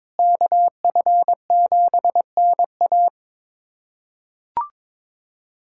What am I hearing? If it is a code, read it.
Morse code, heard as a keyed tone, about 22 wpm, 700 Hz. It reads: KF7NA
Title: USA Callsign DE Callsign - 5 Characters - 22wpm